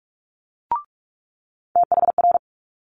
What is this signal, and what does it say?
Morse code, heard as a keyed tone, about 46 wpm, 700 Hz. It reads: THR
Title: Multi-letter Phonemes Quick Fire 45wpm